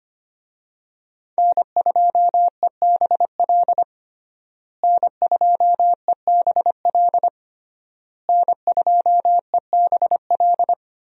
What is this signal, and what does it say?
Morse code, heard as a keyed tone, about 25 wpm, 700 Hz. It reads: N2EBL N2EBL N2EBL
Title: USA Call Signs - 5 Characters - Repeated 3x - 25wpm